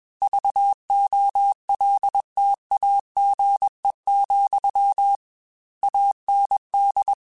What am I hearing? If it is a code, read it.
Morse code, heard as a keyed tone, about 21 wpm, 800 Hz. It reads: VOLTAGE, AND
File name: Technical Subjects / Technical Subjects-mp3s / XC @20